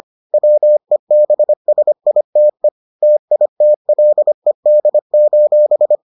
Morse code, heard as a keyed tone, about 25 wpm, 600 Hz. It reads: WEBSITE TITLED: